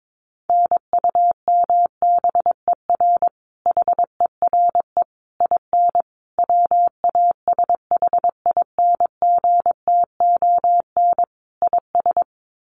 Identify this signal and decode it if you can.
Morse code, heard as a keyed tone, about 22 wpm, 700 Hz. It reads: NUMBER HERE IN WASHINGTON IS
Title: Sets of 5 Words Encore2 22wpm